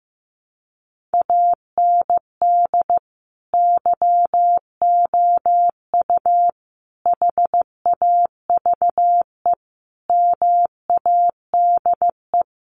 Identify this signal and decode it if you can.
Morse code, heard as a keyed tone, about 15 wpm, 700 Hz. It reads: AND YOU HAVE MADE
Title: Sentences from Top 1500 Words 15wpm